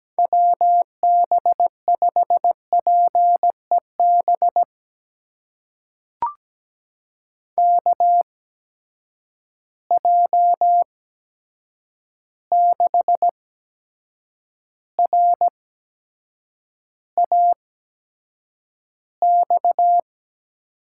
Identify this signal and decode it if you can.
Morse code, heard as a keyed tone, about 17 wpm, spaced as for 7 wpm, 700 Hz. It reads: WB5PEB KJ6RAX
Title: USA Call Signs - 8x Character Spacing - 17wpm